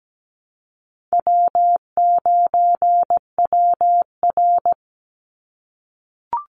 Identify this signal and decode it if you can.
Morse code, heard as a keyed tone, about 17 wpm, 700 Hz. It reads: W9WR
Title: Easy Call Signs - 17wpm